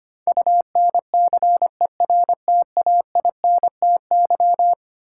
Morse code, heard as a keyed tone, about 25 wpm, 700 Hz. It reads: UNCERTAINTY
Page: morse-code-ninja-practice-25wpm